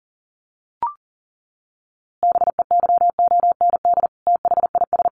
Morse code, heard as a keyed tone, about 40 wpm, 700 Hz. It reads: BEYOND THIS
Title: A Sound Of Thunder 40wpm